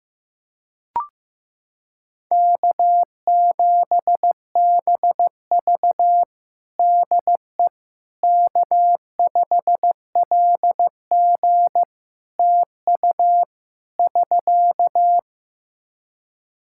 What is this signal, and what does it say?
Morse code, heard as a keyed tone, about 16 wpm, 700 Hz. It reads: K7BV DE K5LG TU <SK>
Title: K1USN SST QSO Practice 15wpm